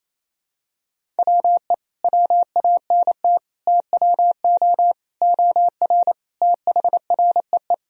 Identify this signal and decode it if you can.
Morse code, heard as a keyed tone, about 28 wpm, 700 Hz. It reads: WE WANT TWO OR THREE
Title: Sentences from Top 1000 Words - Review Part 1 - 28wpm